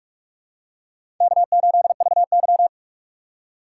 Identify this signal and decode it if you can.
Morse code, heard as a keyed tone, about 46 wpm, 700 Hz. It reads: K8VY
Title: USA Call Signs - 4 Characters - 6x Character Spacing - 45wpm